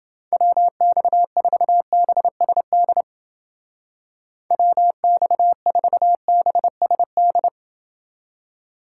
Morse code, heard as a keyed tone, about 29 wpm, 700 Hz. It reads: WX4BSD WX4BSD